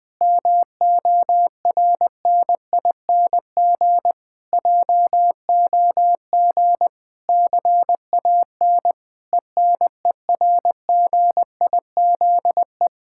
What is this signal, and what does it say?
Morse code, heard as a keyed tone, about 20 wpm, 700 Hz. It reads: MORNING JOG CAN ENERGIZE
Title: Difficult Sentences with Hints - 20wpm